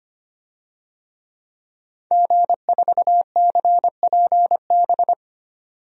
Morse code, heard as a keyed tone, about 25 wpm, 700 Hz. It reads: G4CPB
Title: DXCC-100 Call Signs - 5 Characters - 25wpm